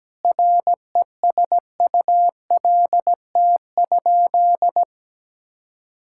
Morse code, heard as a keyed tone, about 17 wpm, 700 Hz. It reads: RESULT?
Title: Sentences from Top 1000 Words - Review Part 5 - 17wpm